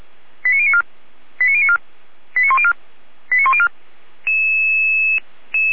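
Five-tone selective call (ZVEI), heard as quick runs of five tones, tones of about 70 ms. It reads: 89094 89094 89104 89104
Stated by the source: ZVEI five-tone selective call received by SDR, from the Feuerwehr Orlamünde fire brigade audio archive.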